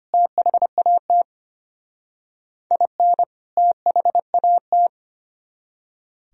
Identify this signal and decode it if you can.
Morse code, heard as a keyed tone, about 33 wpm, 700 Hz. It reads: THAT IN THAT